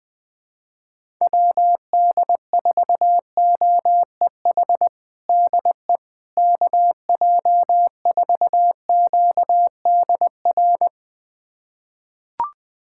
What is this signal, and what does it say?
Morse code, heard as a keyed tone, about 20 wpm, 700 Hz. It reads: WD4OEH DE KJ4QDR
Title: USA Callsign DE Callsign - 6 Characters - 20wpm